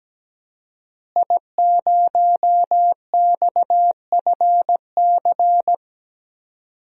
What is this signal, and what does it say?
Morse code, heard as a keyed tone, about 17 wpm, 700 Hz. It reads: I0XFC
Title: DXCC-100 Call Signs - 5 Characters - Repeated 3x - 17wpm